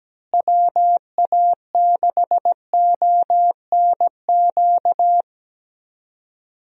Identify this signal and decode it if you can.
Morse code, heard as a keyed tone, about 17 wpm, 700 Hz. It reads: WA6ONQ
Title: USA Call Signs - 6 Characters - Repeated 2x - 17wpm